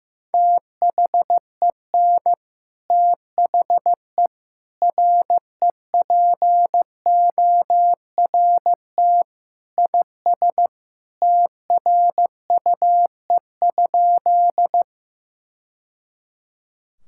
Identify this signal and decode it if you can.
Morse code, heard as a keyed tone, about 15 wpm, 700 Hz. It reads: THEN THE REPORT IS TRUE?
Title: Sentences from Top 1000 Words - Review Part 6 - 15wpm